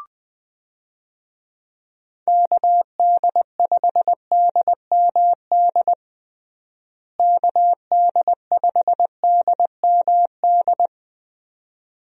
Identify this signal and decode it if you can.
Morse code, heard as a keyed tone, about 20 wpm, 700 Hz. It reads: KD5DMD KD5DMD
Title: USA Call Signs - 6 Characters - Repeated 2x - 20wpm